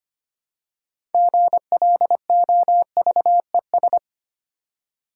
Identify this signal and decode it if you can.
Morse code, heard as a keyed tone, about 25 wpm, 700 Hz. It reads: GLOVES
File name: practice / 25wpm / Difficult Words with Hints - 25wpm